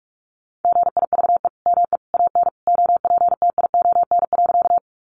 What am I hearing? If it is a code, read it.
Morse code, heard as a keyed tone, about 46 wpm, 700 Hz. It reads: GIVE ME AN OPTION.
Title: Sentences from Top 1300 Words 45wpm